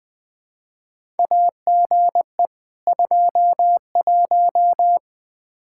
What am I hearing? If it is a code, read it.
Morse code, heard as a keyed tone, about 20 wpm, 700 Hz. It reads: AGE 21
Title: Practice QSO - Exchange 3 - More Info - 20wpm